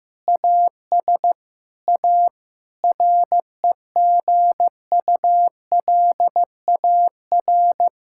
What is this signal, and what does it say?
Morse code, heard as a keyed tone, about 16 wpm, 700 Hz. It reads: AS A REGULAR
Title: Sentences from Top 1000 Words - Review Part 6 - 15wpm